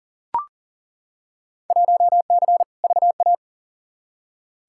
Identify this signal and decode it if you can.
Morse code, heard as a keyed tone, about 40 wpm, 700 Hz. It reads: 1C VA
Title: Field Day Exchanges Encore - 40wpm